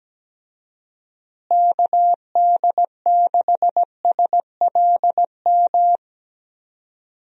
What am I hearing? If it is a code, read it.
Morse code, heard as a keyed tone, about 17 wpm, 700 Hz. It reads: KD6SLM
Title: USA Call Signs - 6 Characters - Repeated 2x - 17wpm